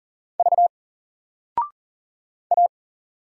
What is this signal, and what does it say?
Morse code, heard as a keyed tone, about 41 wpm, 700 Hz. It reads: V A
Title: Single Character (Letter:Number:Punctuation:Prosign) Alt 40wpm